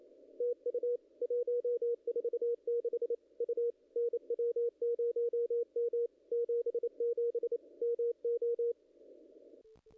Morse code, heard as a keyed tone, about 29 wpm, 440 Hz. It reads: TU 146 U NW0M 77 MO
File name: contests / ss-cw-k5tr-op-n5rz -0000